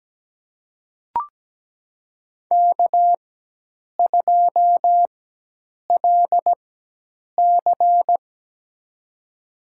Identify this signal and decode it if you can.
Morse code, heard as a keyed tone, about 17 wpm, spaced as for 8 wpm, 700 Hz. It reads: K2LC
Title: USA Call Signs - 4 Characters - 4x Character Spacing - 17wpm